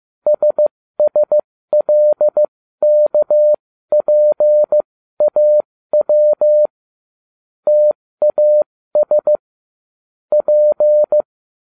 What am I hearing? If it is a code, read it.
Morse code, heard as a keyed tone, about 15 wpm, spaced as for 13 wpm, 600 Hz. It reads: SSLKPAW TAS P